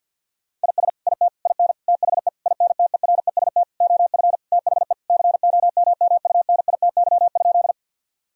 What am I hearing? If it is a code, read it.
Morse code, heard as a keyed tone, about 50 wpm, 700 Hz. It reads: IS IT IN THE INTEREST OF THE COMMUNITY?
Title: Sentences from Top 1000 Words - Review Part 4 - 50wpm